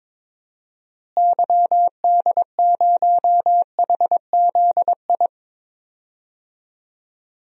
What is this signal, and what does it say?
Morse code, heard as a keyed tone, about 22 wpm, 700 Hz. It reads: YD0HZI
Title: DXCC-100 Call Signs - 6 Characters - Repeated 3x - 22wpm